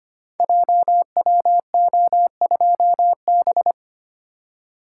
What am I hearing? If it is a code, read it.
Morse code, heard as a keyed tone, about 25 wpm, 700 Hz. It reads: JWO2B
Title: Code Groups - 5 Letter:Numbers - 25wpm